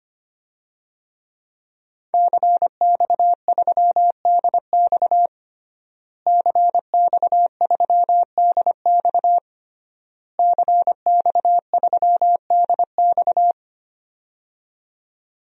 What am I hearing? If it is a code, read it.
Morse code, heard as a keyed tone, about 25 wpm, 700 Hz. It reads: CX3DX CX3DX CX3DX
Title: DXCC-100 Call Signs - 5 Characters - Repeated 3x - 25wpm